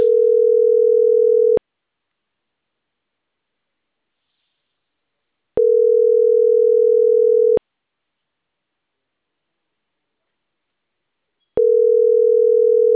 Ringback tone (North America), 440 + 480 Hz.